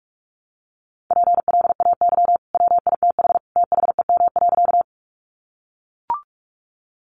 Morse code, heard as a keyed tone, about 44 wpm, 700 Hz. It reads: PLAY WITH THEM.
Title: Sentences from Top 200 Words 45wpm